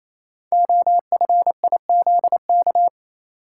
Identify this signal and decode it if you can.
Morse code, heard as a keyed tone, about 27 wpm, 700 Hz. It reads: OFIZK
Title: Code Groups - 5 Letters - 28wpm